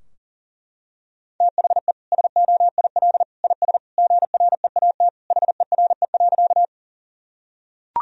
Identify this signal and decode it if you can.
Morse code, heard as a keyed tone, about 40 wpm, 700 Hz. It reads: THE SOIL IS GREAT HERE.
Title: Sentences from Top 1000 Words - Review Part 4 - 40wpm